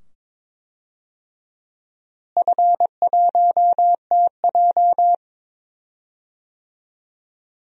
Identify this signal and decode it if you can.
Morse code, heard as a keyed tone, about 21 wpm, 700 Hz. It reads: F1TJ
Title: DXCC-100 Call Signs - 4 Characters - Repeated 2x - 22wpm